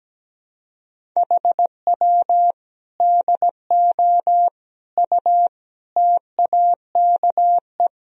Morse code, heard as a keyed tone, about 17 wpm, 700 Hz. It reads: HW DO U TAKE
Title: Sentences from Top 200 Words and Abbreviations 17wpm